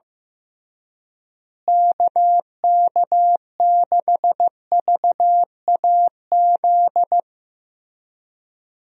Morse code, heard as a keyed tone, about 15 wpm, 700 Hz. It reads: KK6VAZ
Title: USA Call Signs - 6 Characters - Repeated 3x - 15wpm